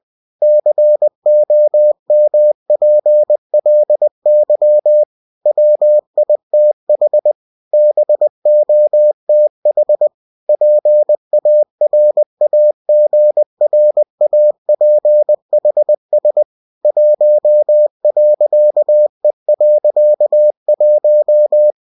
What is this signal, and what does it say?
Morse code, heard as a keyed tone, about 20 wpm, 600 Hz. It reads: COMPLY WITH BOTH PARAGRAPHS 1.E.1
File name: cw-books / winnie-the-pooh / winnie-20wpm-page-0115